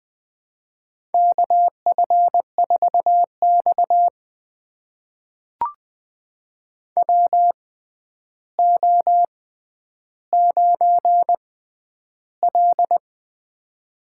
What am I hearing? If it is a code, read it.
Morse code, heard as a keyed tone, about 20 wpm, spaced as for 10 wpm, 700 Hz. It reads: KF4X WO9L
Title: USA Call Signs - 4 Characters - 6x Character Spacing - 20wpm